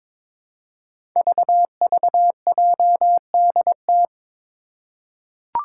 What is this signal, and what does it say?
Morse code, heard as a keyed tone, about 22 wpm, 700 Hz. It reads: VVJDT
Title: Five Challenging Letter-Number Combinations 22wpm